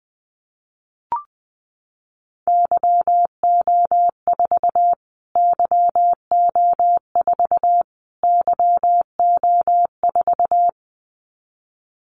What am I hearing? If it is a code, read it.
Morse code, heard as a keyed tone, about 20 wpm, 700 Hz. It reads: YO4 YO4 YO4
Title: Callsign Prefixes for Top 25 DXCC - Repeated 3x - 20wpm